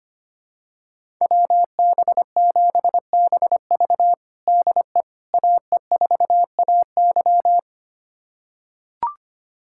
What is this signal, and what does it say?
Morse code, heard as a keyed tone, about 25 wpm, 700 Hz. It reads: WB7BV DE AE4AY
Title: USA Callsign DE Callsign - 5 Characters - 25wpm